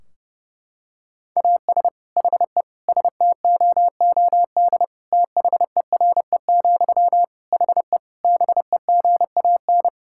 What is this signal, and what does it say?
Morse code, heard as a keyed tone, about 30 wpm, 700 Hz. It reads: AS HE STOOD THERE, HE BEGAN
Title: Sentences from Top 1600 Words 30wpm